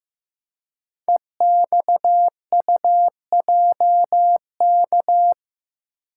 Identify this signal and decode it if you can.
Morse code, heard as a keyed tone, about 15 wpm, 700 Hz. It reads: EXUJK
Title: Code Groups - 5 Letters - 15wpm